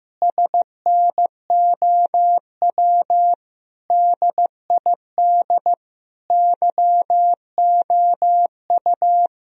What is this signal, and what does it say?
Morse code, heard as a keyed tone, about 15 wpm, 700 Hz. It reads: SNOW DID YOU
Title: Sentences from Top 1000 Words - Review Part 3 - 15wpm